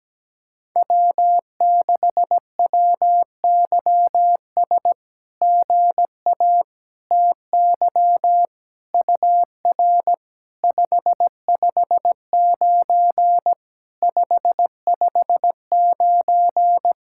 Morse code, heard as a keyed tone, about 17 wpm, 700 Hz. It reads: W6WYS GA TY UR 559 559
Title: SOTA QSO Practice 17wpm